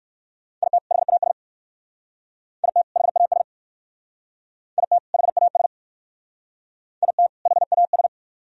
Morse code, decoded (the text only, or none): IT HAS IT HAS IT HAS IT HAS